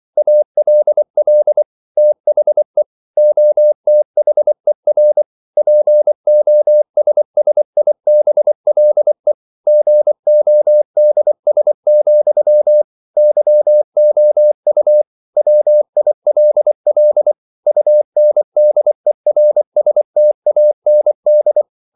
Morse code, heard as a keyed tone, about 24 wpm, 600 Hz. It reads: ALL THE OTHER POSSIBLE GODS, YOU WILL UNDERSTAND
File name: CW / Practice / EnglishQuotations0424